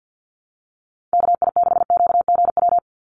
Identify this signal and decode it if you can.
Morse code, heard as a keyed tone, about 50 wpm, 700 Hz. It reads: KI6QGW
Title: USA Call Signs 50wpm